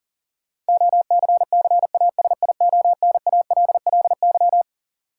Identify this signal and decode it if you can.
Morse code, heard as a keyed tone, about 40 wpm, 700 Hz. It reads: OCCASIONALLY
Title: Sentences from Top 1300 Words - New Word Drill Alt - 40wpm